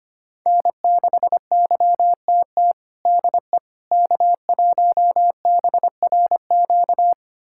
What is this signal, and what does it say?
Morse code, heard as a keyed tone, about 25 wpm, 700 Hz. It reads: N6YTT DE K1BRQ
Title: USA Callsign DE Callsign 25wpm